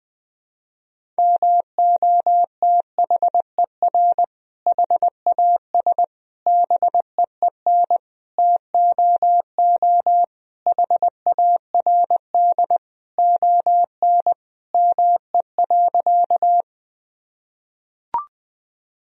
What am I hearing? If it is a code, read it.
Morse code, heard as a keyed tone, about 20 wpm, 700 Hz. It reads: MOTHER HAS BEEN TOO HARD ON ME.
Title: Sentences from Top 1000 Words - Review Part 1 - 20wpm